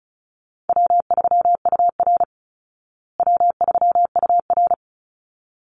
Morse code, heard as a keyed tone, about 34 wpm, 700 Hz. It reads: W3UR W3UR
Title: Call Signs Repeated 3x - 35wpm